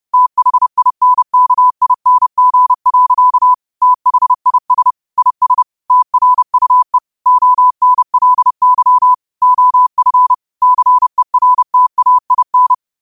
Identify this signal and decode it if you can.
Morse code, heard as a keyed tone, about 30 wpm, 1000 Hz. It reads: THINKING. THIS IS TRUE ONLY OF CERTAIN
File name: qotdfiles / 30wpm